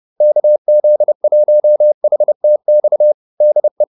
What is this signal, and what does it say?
Morse code, heard as a keyed tone, about 30 wpm, 600 Hz. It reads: KZ1HTX DE
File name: QSO301 30